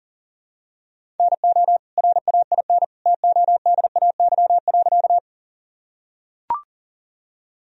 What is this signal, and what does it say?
Morse code, heard as a keyed tone, about 37 wpm, 700 Hz. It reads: NO RAIN TODAY.
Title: Sentences from Top 1000 Words - Review Part 1 - 40wpm